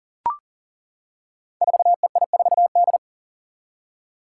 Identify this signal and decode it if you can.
Morse code, heard as a keyed tone, about 40 wpm, 700 Hz. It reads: 4EI4D
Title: Five Challenging Letter-Number Combinations 40wpm